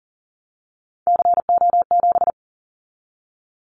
Morse code, heard as a keyed tone, about 39 wpm, 690 Hz. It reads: CO7
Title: Callsign Prefixes for Top 100 DXCC - 40wpm